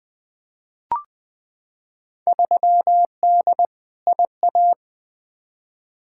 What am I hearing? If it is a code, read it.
Morse code, heard as a keyed tone, about 20 wpm, 700 Hz. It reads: 3D IA
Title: Field Day Exchanges Encore - 20wpm